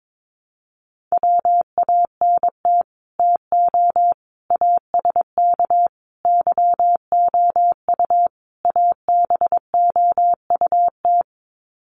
Morse code, heard as a keyed tone, about 22 wpm, 700 Hz. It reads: WANT TO ASK YOU ABOUT
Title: Sets of 5 Words Encore2 22wpm